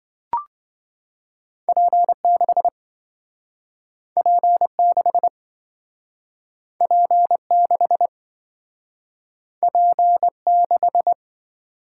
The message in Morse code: P6 P6 P6 P6

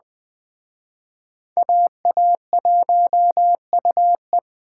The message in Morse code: AA1UE

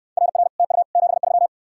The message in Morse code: RR TU 73

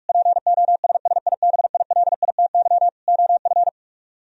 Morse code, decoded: POSSIBILITY OF

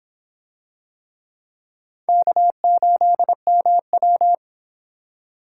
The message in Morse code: K8MW

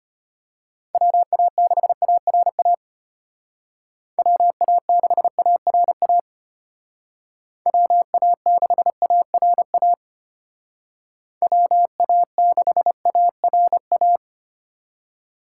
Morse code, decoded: WA6ARA WA6ARA WA6ARA WA6ARA

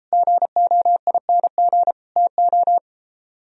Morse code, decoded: GOING TO

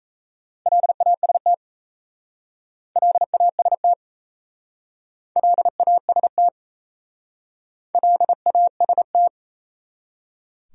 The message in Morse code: LAST LAST LAST LAST

